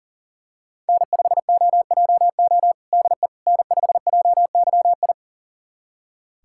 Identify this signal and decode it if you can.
Morse code, heard as a keyed tone, about 40 wpm, 700 Hz. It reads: N5OJO DE N5JYI
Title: USA Callsign DE Callsign - 5 Characters - 40wpm